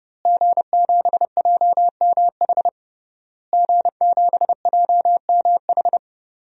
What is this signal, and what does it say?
Morse code, heard as a keyed tone, about 30 wpm, 700 Hz. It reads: G7JMH G7JMH
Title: DXCC-100 Call Signs - 5 Characters - Repeated 3x - 30wpm